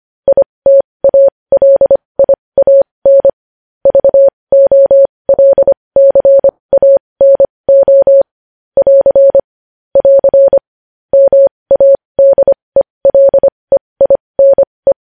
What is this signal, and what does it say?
Morse code, heard as a keyed tone, about 25 wpm, spaced as for 20 wpm, 550 Hz. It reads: ITALIAN VOLCANO <AR> <AR> MADELEINE
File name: podcast / News headlines-1x-25x20